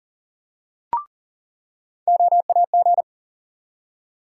Morse code, heard as a keyed tone, about 38 wpm, 700 Hz. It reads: OAG
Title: USA Call Sign Postfixes - 3 Letters 40wpm